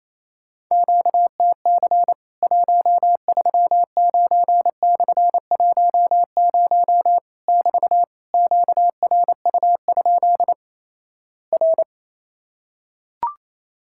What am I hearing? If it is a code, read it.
Morse code, heard as a keyed tone, about 28 wpm, 700 Hz. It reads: QTC 139/10 <BT> QRU? R
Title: Worked All Europe DX Contest - QTC Exchange - 28wpm